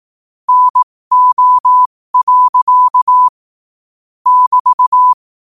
Morse code, heard as a keyed tone, about 18 wpm, spaced as for 15 wpm, 1000 Hz. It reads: NO. <BT>
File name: qotdfiles / 15wpm